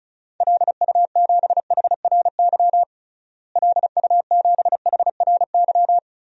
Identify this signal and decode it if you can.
Morse code, heard as a keyed tone, about 34 wpm, 700 Hz. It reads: LU7HRY LU7HRY
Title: DXCC-100 Call Signs - 6 Characters - Repeated 3x - 35wpm